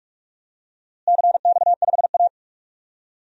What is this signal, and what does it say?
Morse code, heard as a keyed tone, about 43 wpm, 700 Hz. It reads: CX5A
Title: DXCC-100 Call Signs - 4 Characters - Repeated 3x - 45wpm